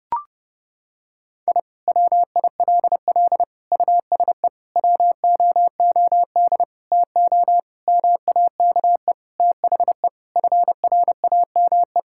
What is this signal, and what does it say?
Morse code, heard as a keyed tone, about 30 wpm, 700 Hz. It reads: I WILL USE WOOD TO MAKE THE FRAME
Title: Sentences from Top 1000 Words - Review Part 6 - 30wpm